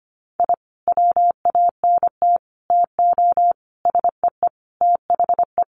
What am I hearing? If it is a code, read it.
Morse code, heard as a keyed tone, about 25 wpm, 700 Hz. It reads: I WANT TO SEE THE
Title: Sentences from Top 1300 Words 25wpm